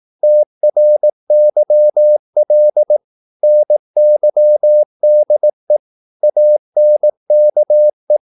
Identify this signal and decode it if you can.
Morse code, heard as a keyed tone, about 18 wpm, 600 Hz. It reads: TRYL NYDE ANKE
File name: Korte ord DK 18wpm